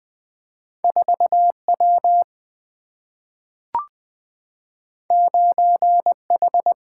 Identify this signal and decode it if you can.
Morse code, heard as a keyed tone, about 20 wpm, 700 Hz. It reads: 4 W 9 H
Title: Single-Letter Single-Number 20wpm